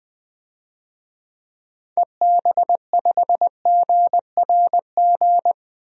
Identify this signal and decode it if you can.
Morse code, heard as a keyed tone, about 20 wpm, 700 Hz. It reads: EB5GRG
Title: DXCC-100 Call Signs - 6 Characters - 20wpm